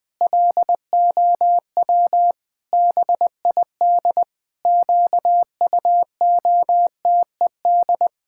Morse code, heard as a keyed tone, about 20 wpm, 700 Hz. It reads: LOW BID QUOTED
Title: Pangram Challenge v2 - 20wpm